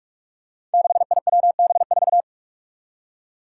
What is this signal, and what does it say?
Morse code, heard as a keyed tone, about 45 wpm, 700 Hz. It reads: 6IWB4
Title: Five Challenging Letter-Number Combinations 45wpm